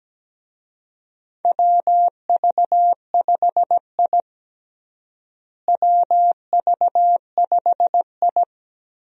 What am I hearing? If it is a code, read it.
Morse code, heard as a keyed tone, about 17 wpm, 700 Hz. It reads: WV5I WV5I
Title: USA Call Signs - 4 Characters - Repeated 2x - 17wpm